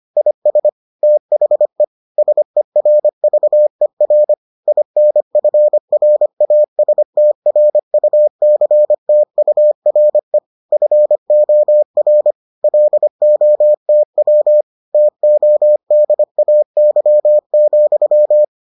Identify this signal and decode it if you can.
Morse code, heard as a keyed tone, about 25 wpm, 600 Hz. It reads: IS THE SERVER INFRASTRUCTURE FOR LOTW TODAY,